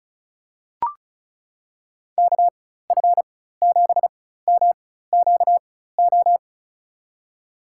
Morse code, heard as a keyed tone, about 34 wpm, spaced as for 16 wpm, 700 Hz. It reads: KF7MQO
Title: USA Call Signs - 6 Characters - 4x Character Spacing - 35wpm